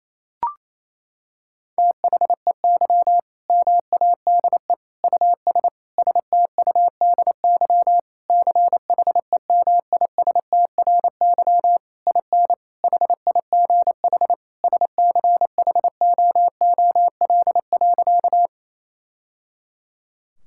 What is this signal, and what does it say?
Morse code, heard as a keyed tone, about 28 wpm, 700 Hz. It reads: THEY MADE US STUDY CHEMISTRY IN HIGH SCHOOL.
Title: Sentences from Top 1000 Words - Review Part 6 - 28wpm